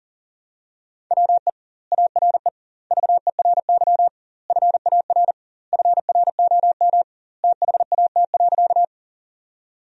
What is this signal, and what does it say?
Morse code, heard as a keyed tone, about 40 wpm, 700 Hz. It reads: WE ARE VERY FAR FROM THAT.
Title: 2x Word-Spacing - Sentences from Top 300 Words - 40wpm